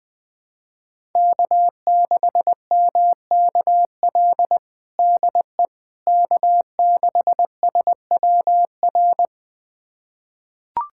Morse code, heard as a keyed tone, about 20 wpm, 700 Hz. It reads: K6MKL DE K6SWR
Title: USA Callsign DE Callsign - 5 Characters - 20wpm